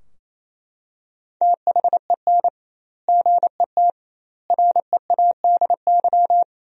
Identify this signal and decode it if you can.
Morse code, heard as a keyed tone, about 28 wpm, 700 Hz. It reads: THEN GET READY
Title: 2x Word-Spacing - Sentences from Top 400 Words - 28wpm